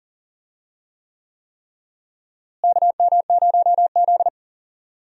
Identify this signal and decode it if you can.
Morse code, heard as a keyed tone, about 40 wpm, 700 Hz. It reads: KM0Z
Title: USA Call Signs - 4 Characters - Repeated 3x - 40wpm